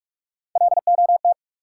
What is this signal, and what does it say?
Morse code, heard as a keyed tone, about 43 wpm, 700 Hz. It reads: LOT
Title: Sentences from Top 500 Words - New Word Drill - 45wpm